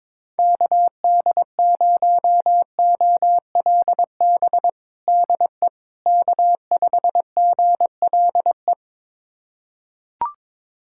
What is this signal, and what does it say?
Morse code, heard as a keyed tone, about 22 wpm, 700 Hz. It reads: KD0OLB DE K5GLE
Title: USA Callsign DE Callsign 22wpm